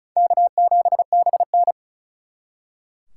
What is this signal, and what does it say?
Morse code, heard as a keyed tone, about 46 wpm, 700 Hz. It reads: K7BN